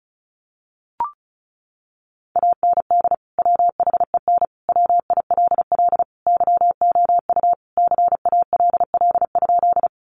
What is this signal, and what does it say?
Morse code, heard as a keyed tone, about 34 wpm, 700 Hz. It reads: AND WHEN WILL YOU CALL?
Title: Sentences from Top 100 Words 35wpm